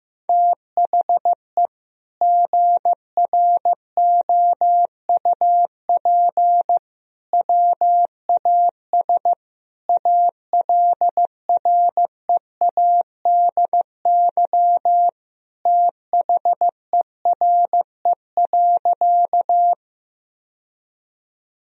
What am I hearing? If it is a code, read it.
Morse code, heard as a keyed tone, about 15 wpm, 700 Hz. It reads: THE GROUP WAS ALREADY THERE.
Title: Sentences from Top 1000 Words - Review Part 1 - 15wpm